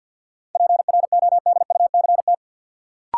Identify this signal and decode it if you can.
Morse code, heard as a keyed tone, about 47 wpm, 700 Hz. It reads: PRODUCT
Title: Sentences from Top 1000 Words - Review Part 2 - New Word Drill - 50wpm